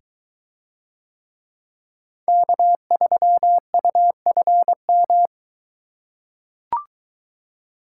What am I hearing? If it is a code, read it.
Morse code, heard as a keyed tone, about 23 wpm, 700 Hz. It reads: K3UFM